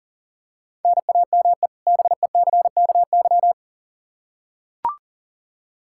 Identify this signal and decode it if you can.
Morse code, heard as a keyed tone, about 37 wpm, 700 Hz. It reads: NAME BECKY
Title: Practice QSO - Exchange 1 - Beginning - 40wpm